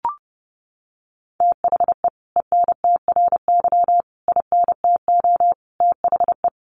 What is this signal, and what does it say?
Morse code, heard as a keyed tone, about 28 wpm, 700 Hz. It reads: THE ENTRY INTO THE
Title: Sentences from Top 1500 Words 30wpm